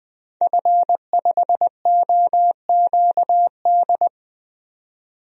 Morse code, heard as a keyed tone, about 20 wpm, 700 Hz. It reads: F5OQD
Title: DXCC-100 Call Signs - 5 Characters - 20wpm